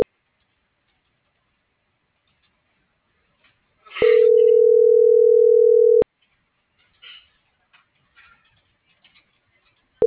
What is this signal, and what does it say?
Ringback tone (North America), 440 + 480 Hz.